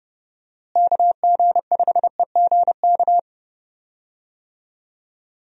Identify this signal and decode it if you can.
Morse code, heard as a keyed tone, about 30 wpm, 700 Hz. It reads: KG5EGK
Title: USA Call Signs - 6 Characters - Repeated 2x - 30wpm